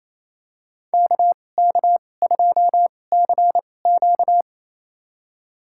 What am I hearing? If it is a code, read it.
Morse code, heard as a keyed tone, about 28 wpm, spaced as for 20 wpm, 700 Hz. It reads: KK2CQ
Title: USA Call Signs - 5 Characters - 2x Character Spacing - 28wpm